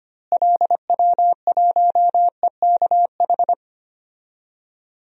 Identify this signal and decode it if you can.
Morse code, heard as a keyed tone, about 25 wpm, 700 Hz. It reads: LW1EKH
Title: DXCC-100 Call Signs - 6 Characters - Repeated 2x - 25wpm